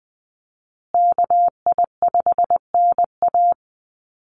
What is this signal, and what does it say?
Morse code, heard as a keyed tone, about 20 wpm, 700 Hz. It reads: KI5NA
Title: USA Call Signs Repeated 3x 20wpm